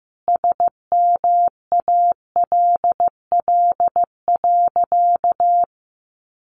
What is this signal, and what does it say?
Morse code, heard as a keyed tone, about 15 wpm, 700 Hz. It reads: SMALL.
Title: Sentences from Top 1500 Words 15wpm